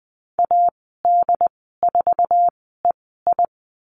Morse code, heard as a keyed tone, about 20 wpm, spaced as for 14 wpm, 700 Hz. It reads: AD4EI
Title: 069 Lesson 12g - Letter d - Call Signs 20wpm